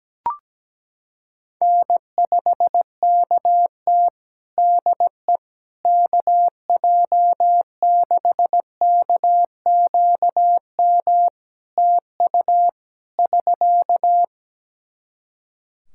Morse code, heard as a keyed tone, about 17 wpm, 700 Hz. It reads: N5KT DE KJ6KQM TU <SK>
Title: Practice QSO - Exchange 4 - Ending - 17wpm